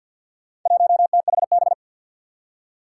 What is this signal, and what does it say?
Morse code, heard as a keyed tone, about 51 wpm, 700 Hz. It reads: 1THB
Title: Four Challenging Letter-Number Combinations 50wpm